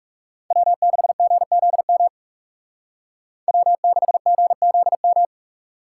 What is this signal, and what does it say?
Morse code, heard as a keyed tone, about 34 wpm, 700 Hz. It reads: W6GZM W6GZM
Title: Speed Racing - USA 1x3 Call Signs 30wpm